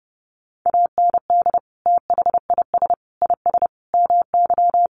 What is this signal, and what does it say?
Morse code, heard as a keyed tone, about 30 wpm, 700 Hz. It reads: AND THIS IS MY
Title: Sentences from Top 1200 Words 30wpm